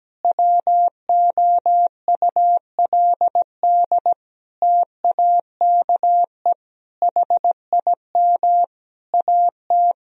Morse code, heard as a keyed tone, about 17 wpm, 700 Hz. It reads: WOULD TAKE HIM AT
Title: Sentences from Top 1000 Words - Review Part 1 - 17wpm